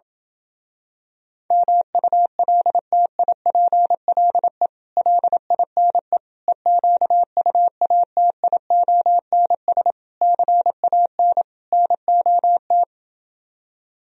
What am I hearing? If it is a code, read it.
Morse code, heard as a keyed tone, about 28 wpm, 700 Hz. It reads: MULTIPLE LINE EQUATIONS CAN NOT